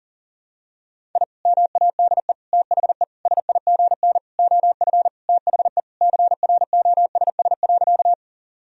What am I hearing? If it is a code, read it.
Morse code, heard as a keyed tone, about 40 wpm, 700 Hz. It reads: I MADE THE SIGN OF THE CROSS.
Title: Sentences from Top 1000 Words - Review Part 1 - 40wpm